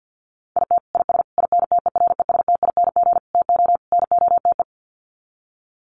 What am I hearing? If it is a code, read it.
Morse code, heard as a keyed tone, about 50 wpm, 700 Hz. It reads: IT IS INTERESTING TO NOTE
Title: Sets of 5 Words Encore2 50wpm